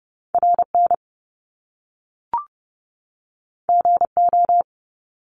Morse code, heard as a keyed tone, about 30 wpm, 700 Hz. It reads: RN GO
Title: Top Two Letter Combinations in Words 30wpm